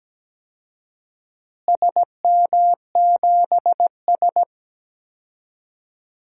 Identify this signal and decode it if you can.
Morse code, heard as a keyed tone, about 17 wpm, 700 Hz. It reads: SM7S
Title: DXCC-100 Call Signs - 4 Characters - Repeated 2x - 17wpm